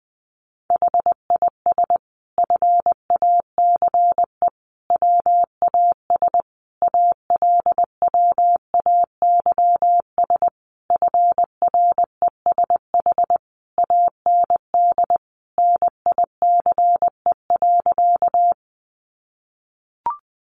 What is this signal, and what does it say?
Morse code, heard as a keyed tone, about 20 wpm, 700 Hz. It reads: HIS FACE WAS ALWAYS FRESH AND NICE.
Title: Sentences from Top 1200 Words 20wpm